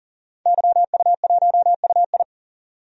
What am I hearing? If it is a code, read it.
Morse code, heard as a keyed tone, about 40 wpm, 700 Hz. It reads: YU1UI
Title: DXCC-100 Call Signs - 5 Characters - 40wpm